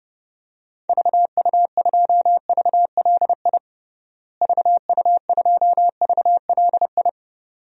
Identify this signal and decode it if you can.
Morse code, heard as a keyed tone, about 30 wpm, 700 Hz. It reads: VU2VLI VU2VLI
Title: DXCC-100 Call Signs - 6 Characters - Repeated 2x - 30wpm